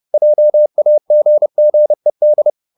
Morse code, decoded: JAGGED

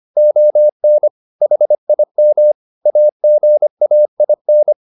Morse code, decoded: ON HIM AGAIN

